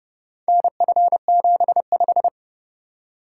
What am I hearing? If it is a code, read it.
Morse code, heard as a keyed tone, about 29 wpm, 700 Hz. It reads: NF75